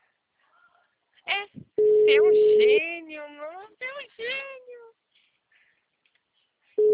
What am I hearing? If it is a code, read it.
Ringback tone (Europe, 425 Hz), with speech.